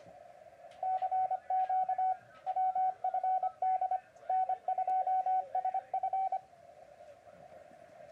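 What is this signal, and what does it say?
Morse code, heard as a keyed tone, about 25 wpm, 780 Hz, with radio noise. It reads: CQ WFD N2SF
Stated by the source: Morse sent with a paddle on a Yaesu FT-991 set up for CW.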